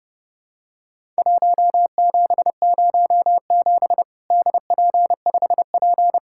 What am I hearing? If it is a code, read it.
Morse code, heard as a keyed tone, about 30 wpm, 700 Hz. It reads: 1707 DP5P
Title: Worked All Europe DX Contest - QTC Exchange - 30wpm